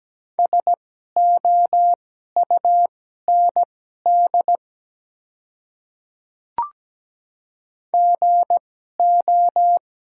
Morse code, heard as a keyed tone, about 17 wpm, spaced as for 12 wpm, 700 Hz. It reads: SOUND GO
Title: 2x Character Spacing - Top 100 Words - 17wpm